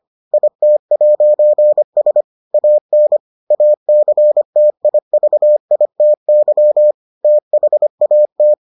Morse code, heard as a keyed tone, about 25 wpm, 600 Hz. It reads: IT'S AN ACTIVITY THAT